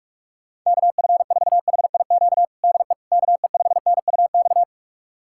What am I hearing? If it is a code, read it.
Morse code, heard as a keyed tone, about 44 wpm, 700 Hz. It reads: KF4HIQ DE KE5NUX
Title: USA Callsign DE Callsign 45wpm